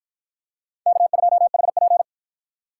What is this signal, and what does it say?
Morse code, heard as a keyed tone, about 53 wpm, 700 Hz. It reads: K2HP